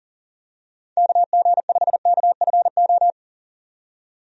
Morse code, heard as a keyed tone, about 40 wpm, 700 Hz. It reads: KG5KFO
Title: USA Call Signs - 6 Characters - 40wpm